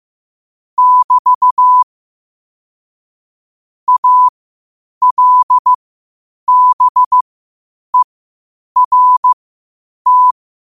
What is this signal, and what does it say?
Morse code, heard as a keyed tone, about 15 wpm, spaced as for 8 wpm, 1000 Hz. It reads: <BT> ALBERT
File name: qotdfiles / 8wpm